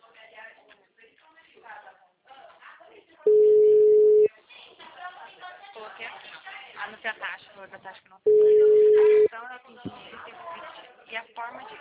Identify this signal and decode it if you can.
Ringback tone (Europe, 425 Hz), with speech.
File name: out- -2006- - - .45769